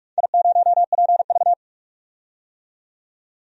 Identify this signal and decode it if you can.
Morse code, heard as a keyed tone, about 44 wpm, 700 Hz. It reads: I0PV
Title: DXCC-100 Call Signs - 4 Characters - 45wpm